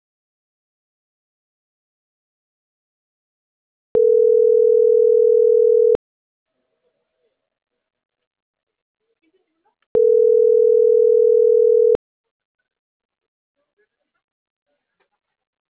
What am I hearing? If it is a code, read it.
Ringback tone (North America), 440 + 480 Hz.